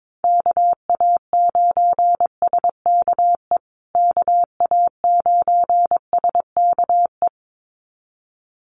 Morse code, heard as a keyed tone, about 22 wpm, 700 Hz. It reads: KA9SKE KA9SKE
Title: USA Call Signs Repeated 2x 22wpm